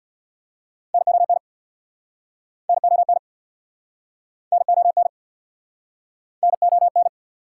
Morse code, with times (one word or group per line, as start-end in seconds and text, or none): 0.9-1.4 NON
2.7-3.2 NON
4.5-5.1 NON
6.4-7.1 NON